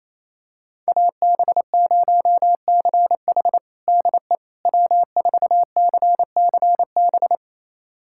Morse code, heard as a keyed tone, about 28 wpm, 700 Hz. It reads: AB0CH DE W4CCB
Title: USA Callsign DE Callsign - 5 Characters - 28wpm